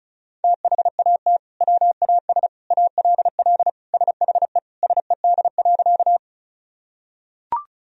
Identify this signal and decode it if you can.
Morse code, heard as a keyed tone, about 36 wpm, 700 Hz. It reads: THAT WAS ALL SHE SED.
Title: Sentences from Top 100 Words and Abbreviations 35wpm